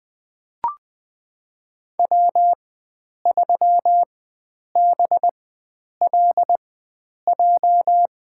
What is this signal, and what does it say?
Morse code, heard as a keyed tone, about 20 wpm, spaced as for 9 wpm, 700 Hz. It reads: W3BLJ
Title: USA Call Signs - 5 Characters - 4x Character Spacing - 20wpm